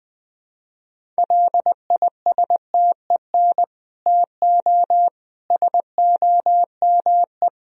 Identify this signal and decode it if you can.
Morse code, heard as a keyed tone, about 20 wpm, 700 Hz. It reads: LISTEN TO SOME
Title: Sentences from Top 1000 Words - Review Part 1 - 20wpm